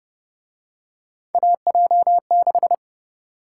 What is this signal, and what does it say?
Morse code, heard as a keyed tone, about 32 wpm, 700 Hz. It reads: AJ6